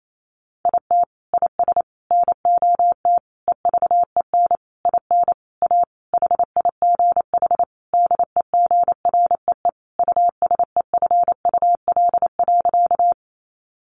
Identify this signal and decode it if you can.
Morse code, heard as a keyed tone, about 28 wpm, 700 Hz. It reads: IT IS NOT EVEN IN A HIGH DEGREE USEFUL.
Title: Sentences from Top 1500 Words 28wpm